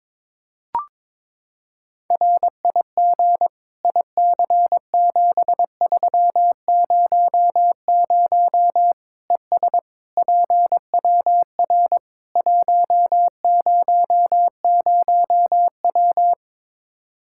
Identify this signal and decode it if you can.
Morse code, heard as a keyed tone, about 21 wpm, 700 Hz. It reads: RIG IC7300 ES PWR 100W
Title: Practice QSO - Exchange 2 - Info - 22wpm